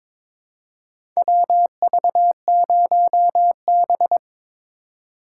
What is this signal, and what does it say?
Morse code, heard as a keyed tone, about 22 wpm, 700 Hz. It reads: WV0B
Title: USA Call Signs - 4 Characters - 4x Character Spacing - 22wpm